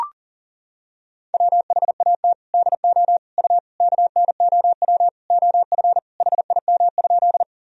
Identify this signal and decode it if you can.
Morse code, heard as a keyed tone, about 38 wpm, 700 Hz. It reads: WHAT DO U KNOW OF HIM?
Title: Sentences from Top 100 Words and Abbreviations 40wpm